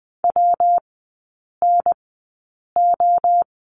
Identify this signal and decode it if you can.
Morse code, heard as a keyed tone, about 20 wpm, spaced as for 8 wpm, 700 Hz. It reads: WNO
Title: 114 Lesson 19e - Letter w - Three Characters 20wpm